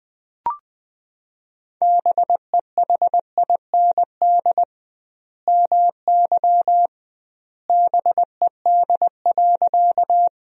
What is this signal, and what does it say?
Morse code, heard as a keyed tone, about 20 wpm, 700 Hz. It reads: BEHIND MY BED.
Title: 2x Word-Spacing - Sentences from Top 500 Words - 20wpm